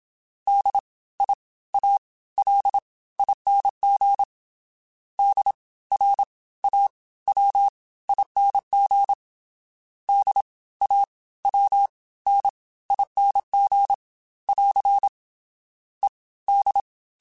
Morse code, heard as a keyed tone, about 27 wpm, spaced as for 17 wpm, 800 Hz. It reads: DIALING DRAWING DAWNING <AR> ED
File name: Special INGs / Special INGs-mp3s / Fast ing @25